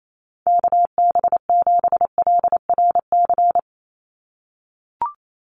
Morse code, heard as a keyed tone, about 28 wpm, 700 Hz. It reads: KB7LRC
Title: USA Call Signs 28wpm